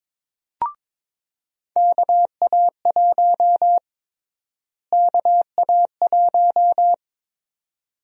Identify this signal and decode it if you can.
Morse code, heard as a keyed tone, about 21 wpm, 700 Hz. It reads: KA1 KA1
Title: USA Call Sign Prefixes - Repeated 2x - 22wpm